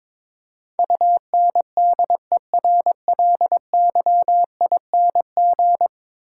Morse code, heard as a keyed tone, about 22 wpm, 700 Hz. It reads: UNDERLYING
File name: practice / 22wpm / 10 Letter Words - First 3 Letters Given - 22wpm